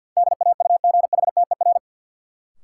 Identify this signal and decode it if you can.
Morse code, heard as a keyed tone, about 48 wpm, 700 Hz. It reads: DAUGHTER
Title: Sentences from Top 1500 Words - New Word Drill Alt - 50wpm